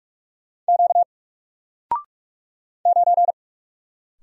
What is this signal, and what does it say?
Morse code, heard as a keyed tone, about 46 wpm, 700 Hz. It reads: Q 9
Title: Single Letter-Number 45wpm